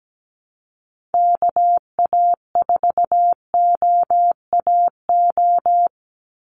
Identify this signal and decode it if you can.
Morse code, heard as a keyed tone, about 17 wpm, 700 Hz. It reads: KA4OAO
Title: USA Call Signs 17wpm